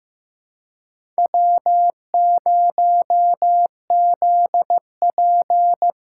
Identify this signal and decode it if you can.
Morse code, heard as a keyed tone, about 15 wpm, 700 Hz. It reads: W0ZP
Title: K1USN SST QSO Practice - 15wpm